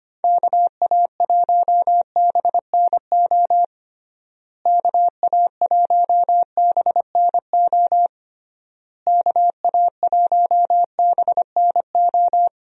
Morse code, heard as a keyed tone, about 25 wpm, 700 Hz. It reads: KA1BNO KA1BNO KA1BNO
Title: USA Call Signs - 6 Characters - Repeated 3x - 25wpm